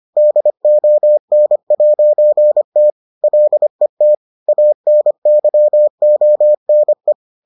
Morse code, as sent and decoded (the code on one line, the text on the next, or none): -.. --- -. .----. - / .-.. - / .- -. -.-- --- -. .
DON'T LET ANYONE